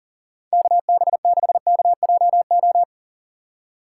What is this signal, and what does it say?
Morse code, heard as a keyed tone, about 48 wpm, 700 Hz. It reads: KB6KJO
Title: Speed Racing - USA 2x3 Call Signs 40wpm